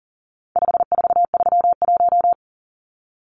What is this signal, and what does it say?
Morse code, heard as a keyed tone, about 40 wpm, 700 Hz. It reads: 5431
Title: Numbers - 4-Digit 40wpm